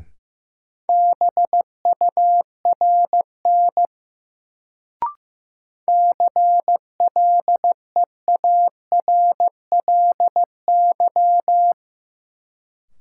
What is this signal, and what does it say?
Morse code, heard as a keyed tone, about 16 wpm, 700 Hz. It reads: BURN CLEARLY
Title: Sentences from Top 1300 Words - New Word Drill Alt - 15wpm